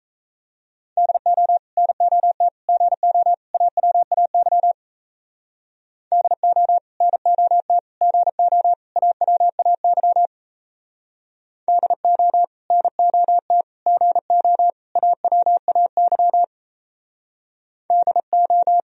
Morse code, heard as a keyed tone, about 34 wpm, 700 Hz. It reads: DO NOT GO AWAY DO NOT GO AWAY DO NOT GO AWAY DO
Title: Speed-Racing - Sets of 4 Words 28wpm